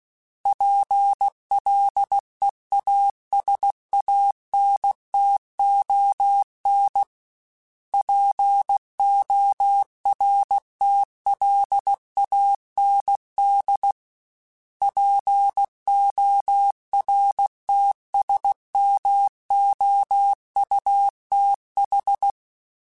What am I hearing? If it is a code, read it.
Morse code, heard as a keyed tone, about 16 wpm, 800 Hz. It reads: PLEASANTON PORTLAND PORTSMOUTH
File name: fromSMs / FromSMs mp3s / Cities@15